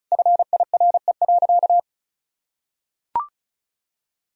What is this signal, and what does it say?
Morse code, heard as a keyed tone, about 34 wpm, 700 Hz. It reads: FIRE.
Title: Sentences from Top 1000 Words - Review Part 1 - 35wpm